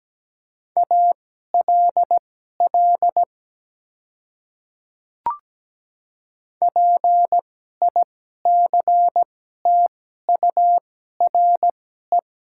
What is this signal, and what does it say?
Morse code, heard as a keyed tone, about 17 wpm, spaced as for 12 wpm, 700 Hz. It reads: ALL PICTURE
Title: 2x Character Spacing - Top 200 Words - 17wpm